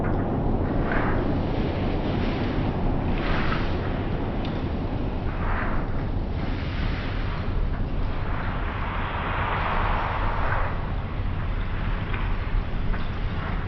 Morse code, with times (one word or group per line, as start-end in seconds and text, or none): none